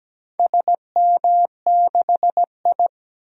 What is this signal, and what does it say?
Morse code, heard as a keyed tone, about 17 wpm, 700 Hz. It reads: SM6I
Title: DXCC-100 Call Signs - 4 Characters - 17wpm